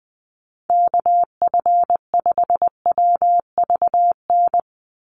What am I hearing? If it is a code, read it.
Morse code, heard as a keyed tone, about 20 wpm, 700 Hz. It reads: KF5WVN
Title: USA Call Signs 20wpm